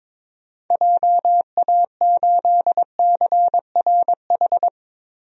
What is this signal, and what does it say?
Morse code, heard as a keyed tone, about 22 wpm, 700 Hz. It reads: JA8CRH
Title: DXCC-100 Call Signs - 6 Characters - Repeated 3x - 22wpm